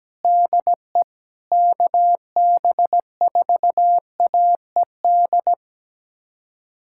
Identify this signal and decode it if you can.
Morse code, heard as a keyed tone, about 17 wpm, 700 Hz. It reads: DE KB4AED
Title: USA Callsign DE Callsign - 6 Characters - 17wpm